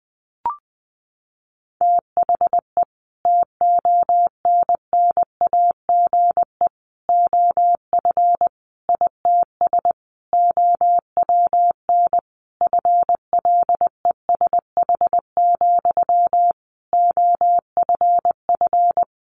Morse code, heard as a keyed tone, about 20 wpm, 700 Hz. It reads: THE TONNAGE OF ITS OWN FLESH, OFF
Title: A Sound Of Thunder 20wpm